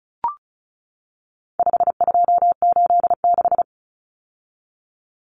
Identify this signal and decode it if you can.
Morse code, heard as a keyed tone, about 34 wpm, 700 Hz. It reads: 5286
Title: Numbers - 4-Digit 35wpm